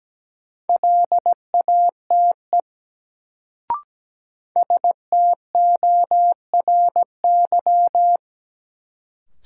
Morse code, heard as a keyed tone, about 17 wpm, 700 Hz. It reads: LATE STORY
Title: Sets of 3 Words - New Word Drill - 17wpm